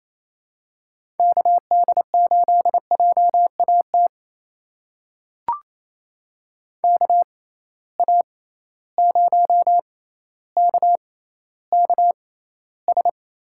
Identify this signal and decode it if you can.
Morse code, heard as a keyed tone, about 28 wpm, spaced as for 14 wpm, 700 Hz. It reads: KD8JAT KA0KKS
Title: USA Call Signs - 6 Characters - 6x Character Spacing - 28wpm